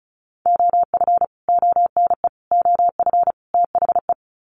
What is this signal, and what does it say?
Morse code, heard as a keyed tone, about 34 wpm, 700 Hz. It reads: OF ONE OF THE
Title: Sets of 4 Words 35wpm